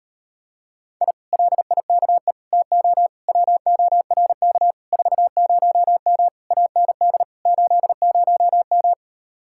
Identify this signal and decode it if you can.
Morse code, heard as a keyed tone, about 39 wpm, 700 Hz. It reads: I LIKE TO WORK 40M AND 80M